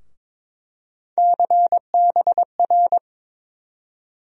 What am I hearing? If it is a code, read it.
Morse code, heard as a keyed tone, about 22 wpm, 700 Hz. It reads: CBR
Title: USA Call Sign Postfixes - 3 Letters 22wpm